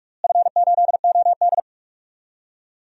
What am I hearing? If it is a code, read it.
Morse code, heard as a keyed tone, about 41 wpm, 690 Hz. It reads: F8OD
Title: DXCC-100 Call Signs - 4 Characters - Repeated 3x - 45wpm